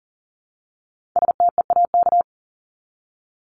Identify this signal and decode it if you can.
Morse code, heard as a keyed tone, about 40 wpm, 700 Hz. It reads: STEAK
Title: Top 1251-1500 Words 40wpm